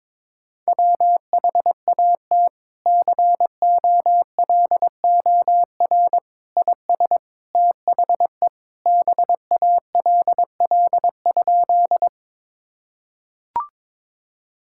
Morse code, heard as a keyed tone, about 22 wpm, 700 Hz. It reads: WHAT COLOR IS THE BALL?
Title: Sentences from Top 1000 Words - Review Part 6 - 22wpm